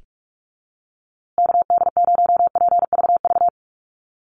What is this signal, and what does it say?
Morse code, heard as a keyed tone, about 44 wpm, 700 Hz. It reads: KD0PVV
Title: USA Call Signs Repeated 3x 45wpm